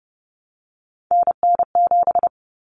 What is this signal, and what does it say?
Morse code, heard as a keyed tone, about 30 wpm, 700 Hz. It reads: NN7
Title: Common Cut Numbers - 3-Digit 30wpm